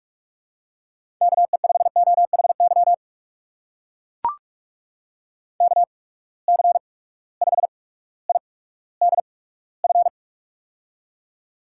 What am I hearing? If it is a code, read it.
Morse code, heard as a keyed tone, about 44 wpm, spaced as for 19 wpm, 700 Hz. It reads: KE5OHY KC5IDF
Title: USA Call Signs - 6 Characters - 8x Character Spacing - 45wpm